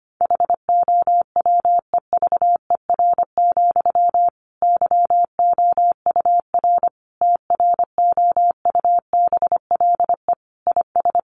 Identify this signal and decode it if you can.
Morse code, heard as a keyed tone, about 25 wpm, 700 Hz. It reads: HOWEVER, YOUR TROUBLE IS
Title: Sentences from Top 1200 Words 25wpm